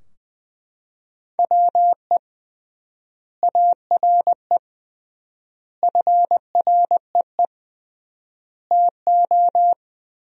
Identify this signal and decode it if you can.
Morse code, heard as a keyed tone, about 21 wpm, 700 Hz. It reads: WE ARE FREE TO
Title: 3x Word-Spacing - Sentences from Top 500 Words - 20wpm